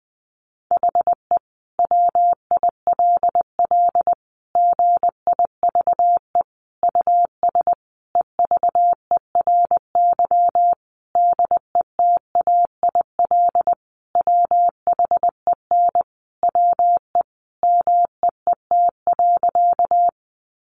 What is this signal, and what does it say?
Morse code, heard as a keyed tone, about 20 wpm, 700 Hz. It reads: HE WILL GIVE US EVERY DETAIL WHEN WE MEET.
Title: Sentences from Top 1500 Words 20wpm